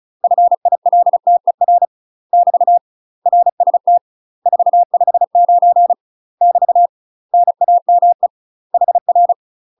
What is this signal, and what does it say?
Morse code, heard as a keyed tone, about 36 wpm, 700 Hz. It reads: FILTER <BT> RST 459 <BT> NAME HR
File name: UK MASTER QSOs TRAINING / 35WPM / UK MASTER QSOs 0014